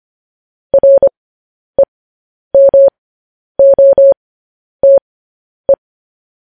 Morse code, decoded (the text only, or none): REMOTE